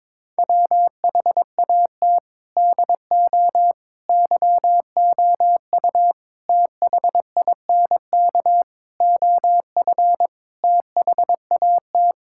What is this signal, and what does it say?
Morse code, heard as a keyed tone, about 22 wpm, 700 Hz. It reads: WHAT DO YOU THINK OF THAT
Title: Sentences from Top 1000 Words - Review Part 6 - 22wpm